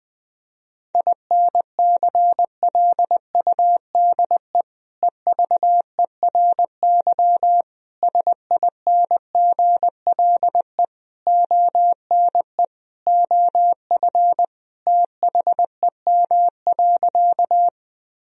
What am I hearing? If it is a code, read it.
Morse code, heard as a keyed tone, about 21 wpm, 700 Hz. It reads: INCLUDE EVERY SINGLE ONE OF THEM.
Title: Sentences from Top 1000 Words - Review Part 4 - 20wpm